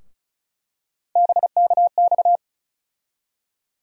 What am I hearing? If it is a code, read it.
Morse code, heard as a keyed tone, about 34 wpm, 700 Hz. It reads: BKX
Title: USA Call Sign Postfixes - 3 Letters 35wpm